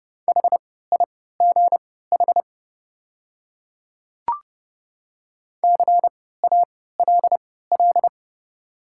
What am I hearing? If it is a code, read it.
Morse code, heard as a keyed tone, about 30 wpm, spaced as for 17 wpm, 700 Hz. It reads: HIGH CALL
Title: 3x Character Spacing - Top 200 Words - 30wpm